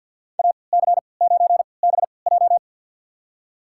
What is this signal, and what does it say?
Morse code, heard as a keyed tone, about 51 wpm, spaced as for 28 wpm, 700 Hz. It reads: AC9BJ
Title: USA Call Signs - 5 Characters - 3x Character Spacing - 50wpm